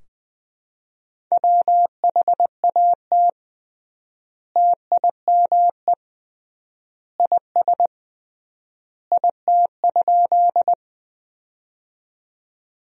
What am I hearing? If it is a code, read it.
Morse code, heard as a keyed tone, about 21 wpm, 700 Hz. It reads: WHAT TIME IS IT?
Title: 3x Word-Spacing - Sentences from Top 100 Words - 20wpm